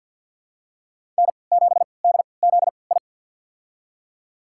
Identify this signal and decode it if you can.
Morse code, heard as a keyed tone, about 49 wpm, spaced as for 28 wpm, 700 Hz. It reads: N7DZI
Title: USA Call Signs - 5 Characters - 3x Character Spacing - 50wpm